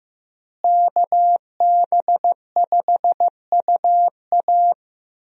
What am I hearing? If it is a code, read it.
Morse code, heard as a keyed tone, about 15 wpm, 700 Hz. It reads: KB5UA
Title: USA Call Signs - 5 Characters - Repeated 3x - 15wpm